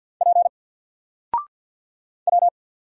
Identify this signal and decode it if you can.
Morse code, heard as a keyed tone, about 54 wpm, 700 Hz. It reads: P W
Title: Single Letter-Number 50wpm